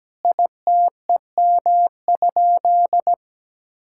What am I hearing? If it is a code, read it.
Morse code, heard as a keyed tone, about 17 wpm, 700 Hz. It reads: ITEM?
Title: Sentences from Top 1000 Words - Review Part 6 - 17wpm